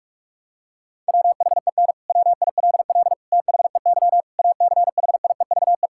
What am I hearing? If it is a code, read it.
Morse code, heard as a keyed tone, about 46 wpm, 700 Hz. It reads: WHEN WILL THEY ACHIEVE
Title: Sentences from Top 1000 Words - Review Part 6 - 45wpm